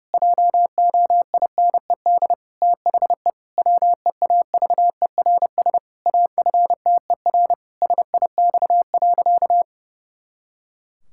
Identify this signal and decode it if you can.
Morse code, heard as a keyed tone, about 30 wpm, 700 Hz. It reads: JOINED THE WEAVERS AFTER SIX.
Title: Pangram Challenge v2 - 30wpm